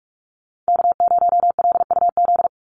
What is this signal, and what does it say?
Morse code, heard as a keyed tone, about 44 wpm, 700 Hz. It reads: K0LUZ
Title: Call Signs - 45wpm